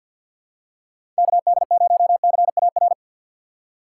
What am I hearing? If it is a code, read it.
Morse code, heard as a keyed tone, about 46 wpm, 700 Hz. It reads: KD0CAR
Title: USA Call Signs - 6 Characters - 3x Character Spacing - 50wpm